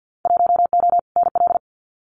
Morse code, heard as a keyed tone, about 50 wpm, 700 Hz. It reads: 1O NL
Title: Winter Field Day Exchanges Encore 50wpm